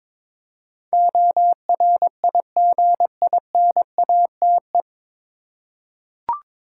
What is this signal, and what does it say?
Morse code, heard as a keyed tone, about 22 wpm, 700 Hz. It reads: ORIGINATE
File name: 9 Letter Words - First 3 Letters Given - 22wpm